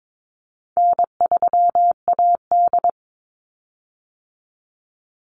Morse code, heard as a keyed tone, about 22 wpm, 700 Hz. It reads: N3AD
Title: Easy Call Signs Repeated 2x - 22wpm